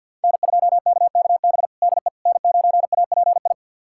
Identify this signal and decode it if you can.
Morse code, heard as a keyed tone, about 50 wpm, 700 Hz. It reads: N2KKB DE N9API
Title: USA Callsign DE Callsign - 5 Characters - 50wpm